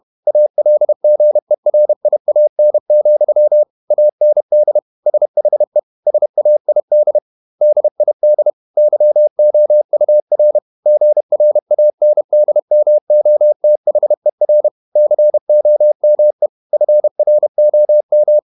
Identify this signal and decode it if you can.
Morse code, heard as a keyed tone, about 31 wpm, 600 Hz. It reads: ALGERIAN, AND SHE SAID DID YOUR GRANDMOTHER COME FROM
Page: Never 31wpm | CW med Gnister